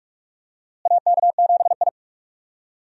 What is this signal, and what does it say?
Morse code, heard as a keyed tone, about 46 wpm, 700 Hz. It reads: AK7I
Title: USA Call Signs - 4 Characters - Repeated 2x - 45wpm